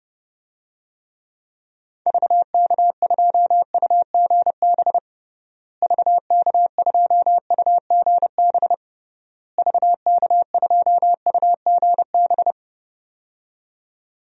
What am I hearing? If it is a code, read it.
Morse code, heard as a keyed tone, about 32 wpm, 700 Hz. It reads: VK2UGB VK2UGB VK2UGB
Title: DXCC-100 Call Signs - 6 Characters - Repeated 3x - 30wpm